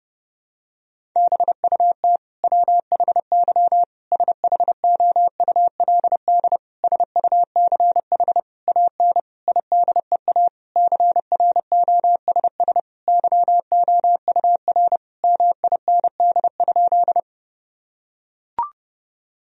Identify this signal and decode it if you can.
Morse code, heard as a keyed tone, about 30 wpm, 700 Hz. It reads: BUT WHY SHOULD SUCH AN IDEA CROSS YOUR MIND?
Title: Sentences from Top 1000 Words - Review Part 1 - 30wpm